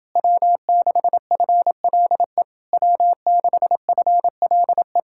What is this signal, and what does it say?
Morse code, heard as a keyed tone, about 28 wpm, 700 Hz. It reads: W6FLE W6FLE